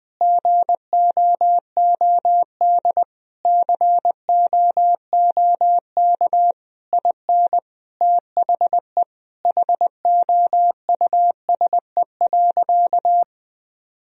Morse code, heard as a keyed tone, about 20 wpm, 700 Hz. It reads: GOOD COOK IN THE HOUSE.
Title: Sentences from Top 1000 Words - Review Part 4 - 20wpm